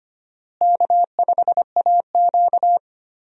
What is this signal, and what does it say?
Morse code, heard as a keyed tone, about 25 wpm, 700 Hz. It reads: K5AQ
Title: USA Call Signs - 4 Characters - Repeated 2x - 25wpm